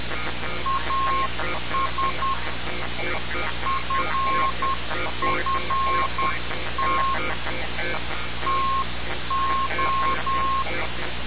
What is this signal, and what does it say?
Morse code, heard as a keyed tone, about 10 wpm, 1070 Hz. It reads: AS FFT TO
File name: radiocap / REC z